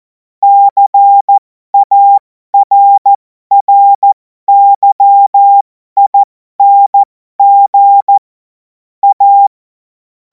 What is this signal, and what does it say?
Morse code, heard as a keyed tone, about 14 wpm, spaced as for 12 wpm, 800 Hz. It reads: CARRYING A
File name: Morsekurs / Witze